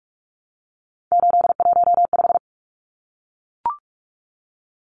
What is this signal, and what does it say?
Morse code, heard as a keyed tone, about 44 wpm, 700 Hz. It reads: 815
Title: Numbers - 3-Digit 45wpm